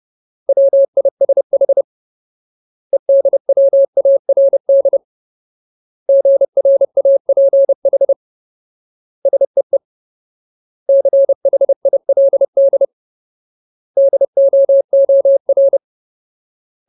Morse code, heard as a keyed tone, about 30 wpm, 550 Hz. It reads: WISH EDWARD GRAPH SEE CHILD DOOR
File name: Ultimate Word List / 30WPM Ultimate